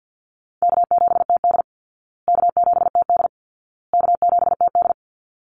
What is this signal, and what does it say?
Morse code, heard as a keyed tone, about 50 wpm, 700 Hz. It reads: K7TD K7TD K7TD
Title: Easy Call Signs Repeated 3x - 50wpm